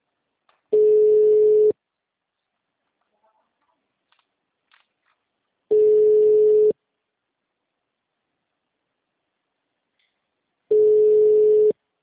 Ringback tone (Europe, 425 Hz).
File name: out- -2006- - - .18